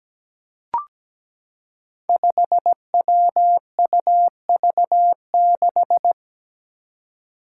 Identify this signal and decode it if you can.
Morse code, heard as a keyed tone, about 17 wpm, 700 Hz. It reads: 5WUV6
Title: Five Challenging Letter-Number Combinations 17wpm